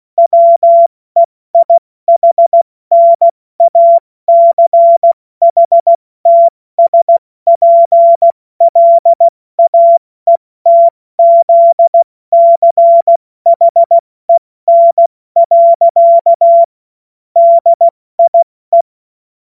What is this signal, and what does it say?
Morse code, heard as a keyed tone, about 16 wpm, 680 Hz. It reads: WEIHNACHTSPLAETZCHEN. DIE
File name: Morsekurs / Witze / Witz 21. März 2023 70 F80